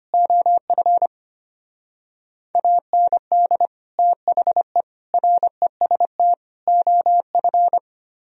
Morse code, heard as a keyed tone, about 30 wpm, 700 Hz. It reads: OF AND THE REST OF
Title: Speed-Racing - Sets of 4 Words 25wpm